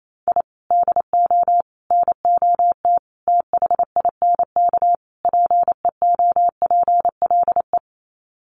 Morse code, heard as a keyed tone, about 28 wpm, 700 Hz. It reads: I DO NOT THINK PEOPLE
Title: Sets of 5 Words Encore2 28wpm